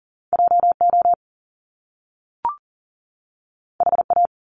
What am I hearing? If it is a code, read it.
Morse code, heard as a keyed tone, about 40 wpm, 700 Hz. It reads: JO HA
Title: Top Two Letter Combinations in Words 40wpm